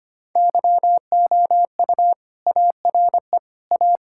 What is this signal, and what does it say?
Morse code, heard as a keyed tone, about 25 wpm, 700 Hz. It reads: YOU ARE A
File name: practice / 25wpm / Sentences with 3-Word Phrases - 25wpm